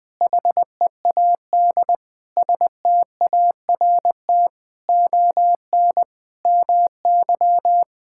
Morse code, heard as a keyed tone, about 20 wpm, 700 Hz. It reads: HEAD START ON MY
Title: Sentences with 7-Word Phrases - Spoken Beginning - 20wpm